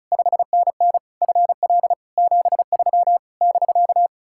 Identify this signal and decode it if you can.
Morse code, heard as a keyed tone, about 45 wpm, 700 Hz. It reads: 5NN FL 73 <BK>